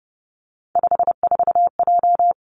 Morse code, heard as a keyed tone, about 30 wpm, 700 Hz. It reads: 54J
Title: Three Challenging Letter-Number Combinations 30wpm